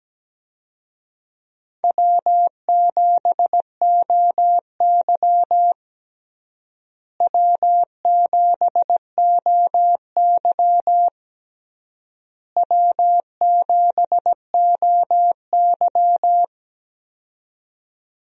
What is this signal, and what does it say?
Morse code, heard as a keyed tone, about 17 wpm, 700 Hz. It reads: W7OY W7OY W7OY
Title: USA Call Signs - 4 Characters - Repeated 3x - 17wpm